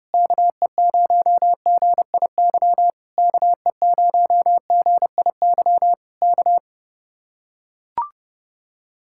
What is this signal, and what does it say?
Morse code, heard as a keyed tone, about 31 wpm, 700 Hz. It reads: KE0GIY KE0GIY K